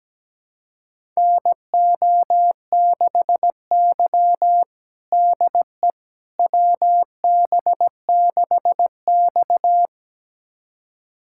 Morse code, heard as a keyed tone, about 17 wpm, 700 Hz. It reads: NO6Y DE WB6X
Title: USA Callsign DE Callsign - 4 Characters - 17wpm